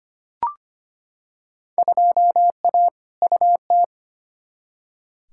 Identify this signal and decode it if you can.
Morse code, heard as a keyed tone, about 25 wpm, 700 Hz. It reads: 2A UT
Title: Field Day Exchanges - 25wpm